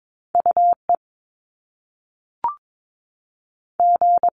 Morse code, heard as a keyed tone, about 22 wpm, 700 Hz. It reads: UE G
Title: Top Two Letter Combinations in Words 22wpm